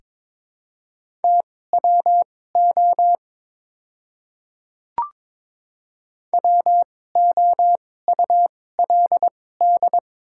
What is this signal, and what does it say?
Morse code, heard as a keyed tone, about 22 wpm, spaced as for 16 wpm, 690 Hz. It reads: TWO WOULD
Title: 2x Character Spacing - Top 100 Words - 22wpm